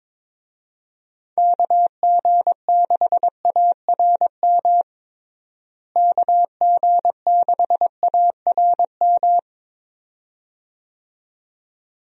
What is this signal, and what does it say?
Morse code, heard as a keyed tone, about 22 wpm, 700 Hz. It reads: KG6ARM KG6ARM
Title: USA Call Signs - 6 Characters - Repeated 2x - 22wpm